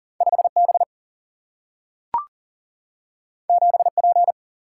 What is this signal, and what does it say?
Morse code, heard as a keyed tone, about 42 wpm, 700 Hz. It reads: 5B 7P
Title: Single-Letter Single-Number 40wpm